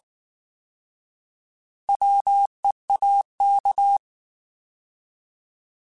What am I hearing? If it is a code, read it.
Morse code, heard as a keyed tone, about 19 wpm, 800 Hz. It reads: WEAK
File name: Alphabetical / Alphabetical mp3s / Ws @18